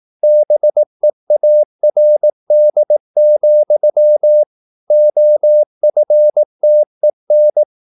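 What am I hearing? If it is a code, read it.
Morse code, heard as a keyed tone, about 18 wpm, 600 Hz. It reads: BEARD, OFTEN
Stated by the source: Morse practice at 18 wpm, English text, from CW med Gnister.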